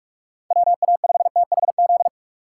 Morse code, heard as a keyed tone, about 40 wpm, 700 Hz. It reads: WA5THZ